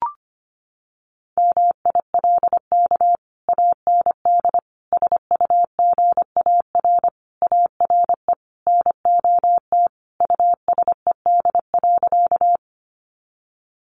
Morse code, heard as a keyed tone, about 25 wpm, 700 Hz. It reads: MILK AND SUGAR ARE NOT USED.
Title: Sentences from Top 1500 Words 25wpm